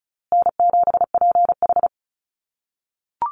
Morse code, heard as a keyed tone, about 34 wpm, 700 Hz. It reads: N7PH
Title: USA Call Signs Repeated 3x 35wpm